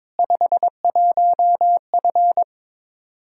Morse code, heard as a keyed tone, about 21 wpm, 700 Hz. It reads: 51F
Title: Practice QSO - Exchange 2 - Info - 22wpm